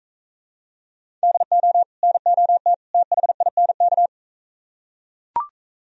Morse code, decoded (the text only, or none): DO NOT THINK